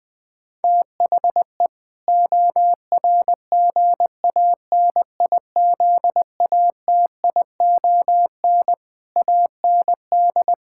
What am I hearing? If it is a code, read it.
Morse code, heard as a keyed tone, about 20 wpm, 700 Hz. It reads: THE ORGANIZATION AND
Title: Sentences from Top 1000 Words - Review Part 6 - 20wpm